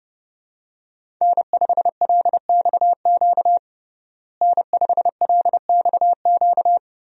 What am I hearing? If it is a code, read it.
Morse code, heard as a keyed tone, about 31 wpm, 700 Hz. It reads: N5LXQ N5LXQ
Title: USA Call Signs - 5 Characters - Repeated 2x - 30wpm